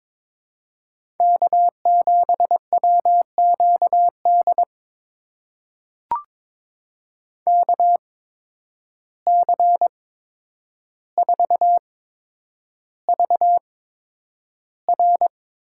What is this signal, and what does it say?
Morse code, heard as a keyed tone, about 21 wpm, spaced as for 9 wpm, 700 Hz. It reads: K7WQD KC4VR
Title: USA Call Signs - 5 Characters - 8x Character Spacing - 22wpm